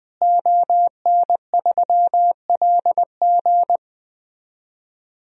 Morse code, heard as a keyed tone, about 20 wpm, 700 Hz. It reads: ON3LG
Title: DXCC-100 Call Signs - 5 Characters - Repeated 2x - 20wpm